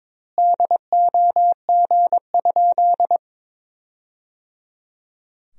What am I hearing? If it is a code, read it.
Morse code, heard as a keyed tone, about 22 wpm, 700 Hz. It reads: DOG?
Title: 2x Word-Spacing - Sentences from Top 400 Words - 22wpm